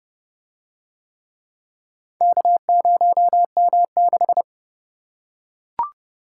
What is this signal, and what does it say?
Morse code, heard as a keyed tone, about 31 wpm, 700 Hz. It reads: K0M6